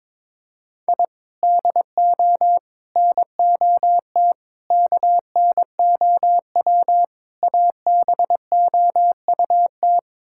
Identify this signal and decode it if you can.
Morse code, heard as a keyed tone, about 22 wpm, 700 Hz. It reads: I DO NOT KNOW ABOUT
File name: practice / 22wpm / Build A Phrase - 5-Word Phrases - 22wpm